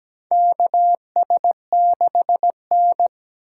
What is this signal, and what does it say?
Morse code, heard as a keyed tone, about 17 wpm, 700 Hz. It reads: KS6N
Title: USA Call Signs - 4 Characters - 3x Character Spacing - 17wpm